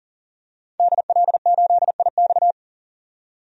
Morse code, heard as a keyed tone, about 40 wpm, 700 Hz. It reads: DL8IX
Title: DXCC-100 Call Signs - 5 Characters - 40wpm